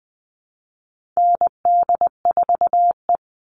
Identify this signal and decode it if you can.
Morse code, heard as a keyed tone, about 20 wpm, 700 Hz. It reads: ND4E
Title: All Cut Numbers - 4-Digit 20wpm